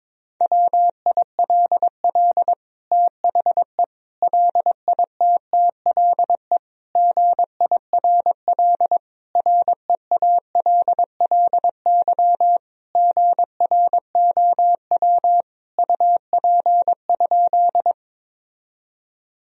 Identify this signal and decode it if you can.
Morse code, heard as a keyed tone, about 22 wpm, 700 Hz. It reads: WILL THE LITTLE GIRL REALLY GROW UP?
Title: Sentences from Top 1000 Words - Review Part 1 - 22wpm